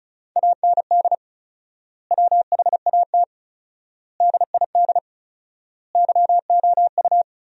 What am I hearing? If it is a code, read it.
Morse code, heard as a keyed tone, about 36 wpm, 700 Hz. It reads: AND WHAT DID YOU
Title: 4x Word-Spacing - Sentences from Top 100 Words - 35wpm